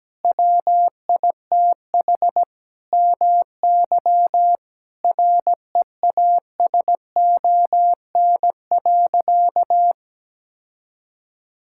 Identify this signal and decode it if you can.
Morse code, heard as a keyed tone, about 17 wpm, 700 Hz. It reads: WITH MY REASON.
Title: Sentences from Top 1000 Words - Review Part 5 - 17wpm